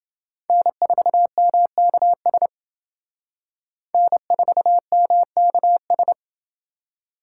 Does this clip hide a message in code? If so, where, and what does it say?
Yes: Morse code from 0.5 to 6.1 s, heard as a keyed tone, about 29 wpm, 700 Hz. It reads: N4MKS N4MKS